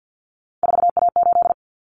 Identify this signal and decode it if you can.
Morse code, heard as a keyed tone, about 48 wpm, 700 Hz. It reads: 4A8
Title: Common Cut Numbers - 3-Digit 50wpm